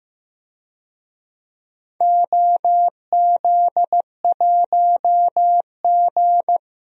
Morse code, heard as a keyed tone, about 15 wpm, 700 Hz. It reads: OZ1G
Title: DXCC-100 Call Signs - 4 Characters - 15wpm